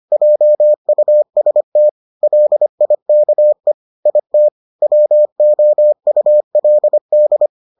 Morse code, heard as a keyed tone, about 25 wpm, 600 Hz. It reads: JUST LIKE IT WOULD